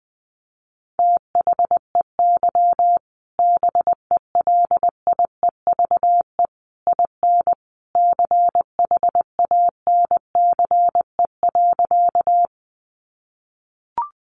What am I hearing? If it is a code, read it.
Morse code, heard as a keyed tone, about 19 wpm, 700 Hz. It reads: THEY BELIEVE IN CHANCE.
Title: Sentences from Top 1000 Words 20wpm